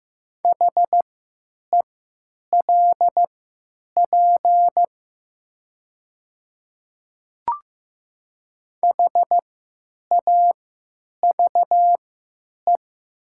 Morse code, heard as a keyed tone, about 15 wpm, spaced as for 9 wpm, 700 Hz. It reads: HELP HAVE
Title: 3x Character Spacing - Top 200 Words - 15wpm